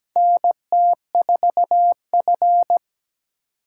Morse code, heard as a keyed tone, about 17 wpm, 700 Hz. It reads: NT4F
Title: USA Call Signs - 4 Characters - 2x Character Spacing - 17wpm